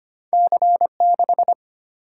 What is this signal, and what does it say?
Morse code, heard as a keyed tone, about 25 wpm, 700 Hz. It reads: C6
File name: Two Letter-Number Combinations - 25wpm